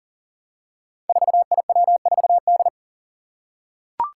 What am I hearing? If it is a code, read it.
Morse code, heard as a keyed tone, about 40 wpm, 700 Hz. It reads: 4IW4D
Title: Five Challenging Letter-Number Combinations 40wpm